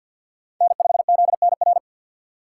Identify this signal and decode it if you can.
Morse code, heard as a keyed tone, about 51 wpm, 700 Hz. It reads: N5ZNR
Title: USA Call Signs - 5 Characters - 2x Character Spacing - 50wpm